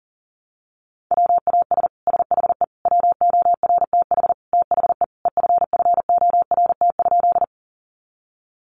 Morse code, heard as a keyed tone, about 37 wpm, 700 Hz. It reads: WAS SHE WORTH THE EFFORT?
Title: Sentences from Top 1200 Words 40wpm